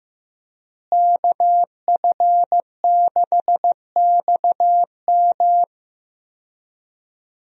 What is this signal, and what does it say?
Morse code, heard as a keyed tone, about 15 wpm, 700 Hz. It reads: KF6XM
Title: USA Call Signs - 5 Characters - Repeated 3x - 15wpm